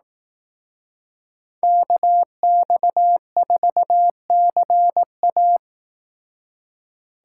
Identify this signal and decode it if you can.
Morse code, heard as a keyed tone, about 18 wpm, 700 Hz. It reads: KX4CA
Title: Speed Racing - USA 2x2 Call Signs 15wpm